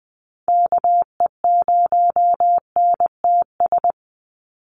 Morse code, heard as a keyed tone, about 20 wpm, 700 Hz. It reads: KE0NTS
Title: USA Call Signs 20wpm